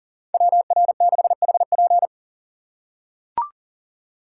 Morse code, heard as a keyed tone, about 40 wpm, 700 Hz. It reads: WR6HP
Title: USA Call Signs - 5 Characters - 2x Character Spacing - 40wpm